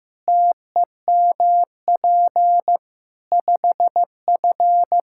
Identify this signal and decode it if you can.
Morse code, heard as a keyed tone, about 16 wpm, 700 Hz. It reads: TEMP 5F
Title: Practice QSO - Exchange 2 - Info - 15wpm